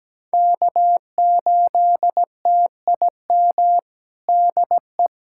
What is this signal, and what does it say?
Morse code, heard as a keyed tone, about 17 wpm, 700 Hz. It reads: K8TIM DE
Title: USA Callsign DE Callsign - 5 Characters - 17wpm